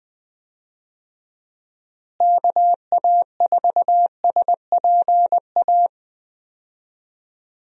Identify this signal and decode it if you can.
Morse code, heard as a keyed tone, about 20 wpm, 700 Hz. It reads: KA4SPA
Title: USA Call Signs - 6 Characters - Repeated 2x - 20wpm